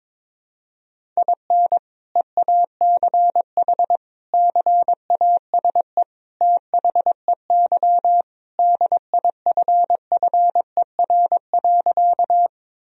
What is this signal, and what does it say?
Morse code, heard as a keyed tone, about 22 wpm, 700 Hz. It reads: IN EACH CASE THEY DIFFER.
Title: Sentences from Top 1000 Words - Review Part 1 - 22wpm